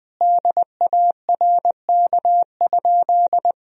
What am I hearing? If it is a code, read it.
Morse code, heard as a keyed tone, about 19 wpm, 700 Hz. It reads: DARK?
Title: Sentences from Top 1000 Words - Review Part 1 - 20wpm